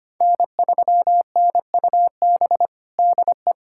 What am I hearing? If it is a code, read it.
Morse code, heard as a keyed tone, about 25 wpm, 700 Hz. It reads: N3NUB DE
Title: USA Callsign DE Callsign - 5 Characters - 25wpm